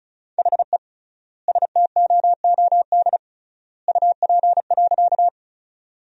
Morse code, heard as a keyed tone, about 36 wpm, 700 Hz. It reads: HE STOOD UP.
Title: 3x Word-Spacing - Sentences from Top 500 Words - 35wpm